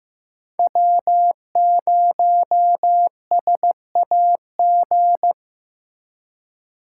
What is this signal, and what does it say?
Morse code, heard as a keyed tone, about 18 wpm, 700 Hz. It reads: W0SAG
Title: Speed Racing - USA 1x3 Call Signs 15wpm